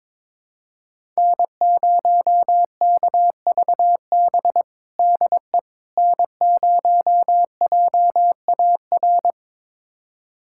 Morse code, heard as a keyed tone, about 22 wpm, 700 Hz. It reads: N0KVB DE N0JAR
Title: USA Callsign DE Callsign - 5 Characters - 22wpm